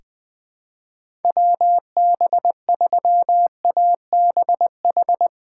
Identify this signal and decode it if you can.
Morse code, heard as a keyed tone, about 20 wpm, 700 Hz. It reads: WB3ABH
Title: USA Call Signs - 6 Characters - 3x Character Spacing - 20wpm